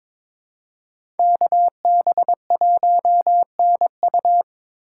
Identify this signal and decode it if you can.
Morse code, heard as a keyed tone, about 22 wpm, 700 Hz. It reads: KB1NU
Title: USA Call Signs - 5 Characters - 4x Character Spacing - 22wpm